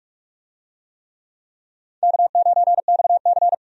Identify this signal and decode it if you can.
Morse code, heard as a keyed tone, about 41 wpm, 700 Hz. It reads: K9XC
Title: USA Call Signs - 4 Characters - 45wpm